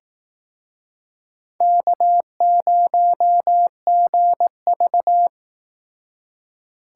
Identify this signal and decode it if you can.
Morse code, heard as a keyed tone, about 18 wpm, 700 Hz. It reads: K0GV
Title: Speed Racing - USA Call Signs 15wpm